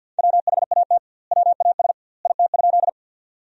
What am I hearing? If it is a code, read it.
Morse code, heard as a keyed tone, about 48 wpm, 700 Hz. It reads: WHAT WAS IT?
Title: 2x Word-Spacing - Sentences from Top 100 Words - 50wpm